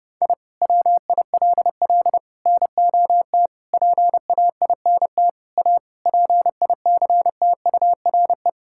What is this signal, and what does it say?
Morse code, heard as a keyed tone, about 30 wpm, 700 Hz. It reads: I WILL NOT PAINT A PICTURE
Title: Sentences from Top 1000 Words - Review Part 6 - 30wpm